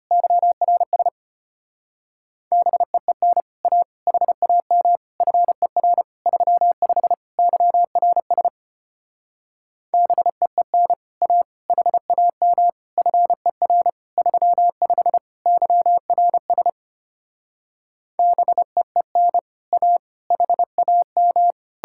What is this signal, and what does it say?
Morse code, heard as a keyed tone, about 33 wpm, 700 Hz. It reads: YRS BEEN A HAM FER 35 YRS BEEN A HAM FER 35 YRS BEEN A HAM